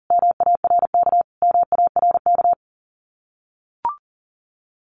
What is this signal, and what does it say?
Morse code, heard as a keyed tone, about 37 wpm, 700 Hz. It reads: MARK MARK
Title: Common QSO Phrases Encore 40wpm